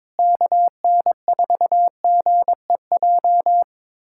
Morse code, heard as a keyed tone, about 21 wpm, 700 Hz. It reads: KN4GEJ
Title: USA Callsign DE Callsign 22wpm